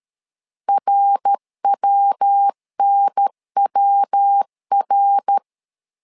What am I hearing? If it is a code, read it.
Morse code, heard as a keyed tone, about 13 wpm, 800 Hz. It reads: RWNWR
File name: LEZIONI / 3 MWRO / 3 MWRO 12wpm 1730